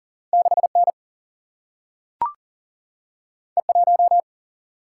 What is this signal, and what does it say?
Morse code, heard as a keyed tone, about 41 wpm, 700 Hz. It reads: 6N E1
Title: Single-Letter Single-Number 40wpm